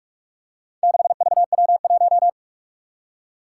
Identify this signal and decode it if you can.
Morse code, heard as a keyed tone, about 44 wpm, 700 Hz. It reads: 6VW1
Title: Four Challenging Letter-Number Combinations 45wpm